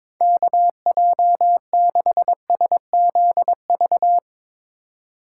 Morse code, heard as a keyed tone, about 24 wpm, 700 Hz. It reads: KJ6SZV
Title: Speed Racing - USA 2x3 Call Signs 22wpm